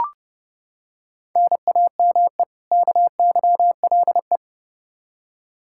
Morse code, heard as a keyed tone, about 30 wpm, 700 Hz. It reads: NAME KYLE
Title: Practice QSO - Exchange 1 - Beginning - 30wpm